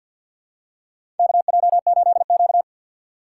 Morse code, heard as a keyed tone, about 53 wpm, 700 Hz. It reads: KJ8Q
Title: USA Call Signs - 4 Characters - 2x Character Spacing - 50wpm